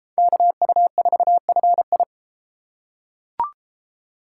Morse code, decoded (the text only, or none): KU4FI